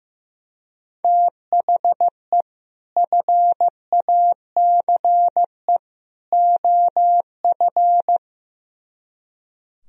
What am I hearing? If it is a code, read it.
Morse code, heard as a keyed tone, about 15 wpm, 700 Hz. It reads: THE FACE OF
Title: Speed-Racing - Sets of 3 Words 15wpm